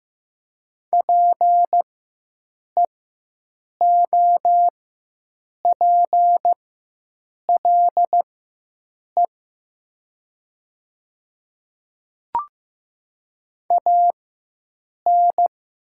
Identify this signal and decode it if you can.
Morse code, heard as a keyed tone, about 15 wpm, spaced as for 7 wpm, 700 Hz. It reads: PEOPLE AN
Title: 4x Character Spacing - Top 100 Words - 15wpm